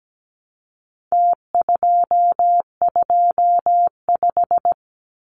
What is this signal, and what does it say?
Morse code, heard as a keyed tone, about 17 wpm, 700 Hz. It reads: T225
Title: Numbers - Contest Style - 4-Digit 17wpm